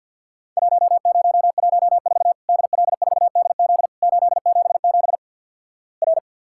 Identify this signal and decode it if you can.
Morse code, heard as a keyed tone, about 48 wpm, 700 Hz. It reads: 1014 DL4DZ 877 R
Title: Worked All Europe DX Contest - QTC Exchange with Repeats - 50wpm